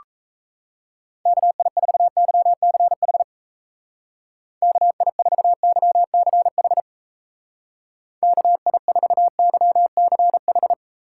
Morse code, heard as a keyed tone, about 36 wpm, 700 Hz. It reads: KI4YCH KI4YCH KI4YCH